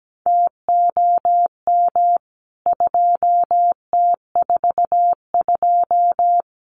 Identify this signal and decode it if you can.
Morse code, heard as a keyed tone, about 17 wpm, 700 Hz. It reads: TOM 2T42
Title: CWT Exchange 17wpm